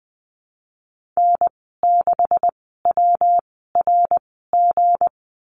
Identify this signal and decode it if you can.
Morse code, heard as a keyed tone, about 19 wpm, spaced as for 14 wpm, 700 Hz. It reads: N6WRG
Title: 161 Lesson 26g - Letter g - Call Signs 20wpm